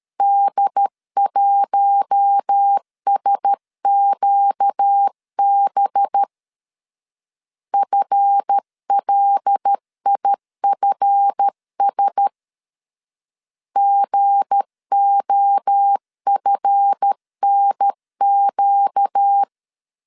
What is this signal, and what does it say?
Morse code, heard as a keyed tone, about 13 wpm, 800 Hz. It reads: D1SQB FLIFS GOFNQ
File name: LEZIONI / 7 GF9Z / 7 GF 12wpm 2122